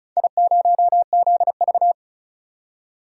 Morse code, heard as a keyed tone, about 36 wpm, 700 Hz. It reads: I0ZV
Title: DXCC-100 Call Signs - 4 Characters - 35wpm